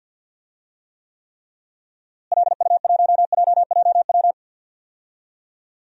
Morse code, heard as a keyed tone, about 46 wpm, 700 Hz. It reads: LU1JJW
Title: DXCC-100 Call Signs - 6 Characters - Repeated 3x - 50wpm